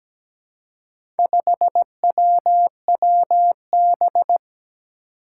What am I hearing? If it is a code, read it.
Morse code, heard as a keyed tone, about 17 wpm, 700 Hz. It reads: 5WWB
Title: Four Challenging Letter-Number Combinations 17wpm